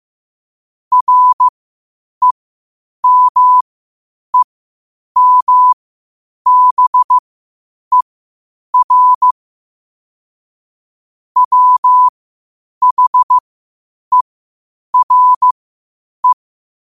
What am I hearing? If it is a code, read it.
Morse code, heard as a keyed tone, about 15 wpm, spaced as for 8 wpm, 1000 Hz. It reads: REMEMBER WHERE
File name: qotdfiles / 8wpm